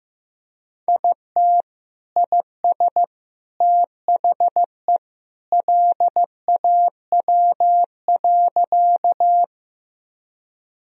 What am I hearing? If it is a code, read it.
Morse code, heard as a keyed tone, about 15 wpm, 700 Hz. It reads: IT IS THE LAW.
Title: Sentences from Top 1000 Words - Review Part 3 - 15wpm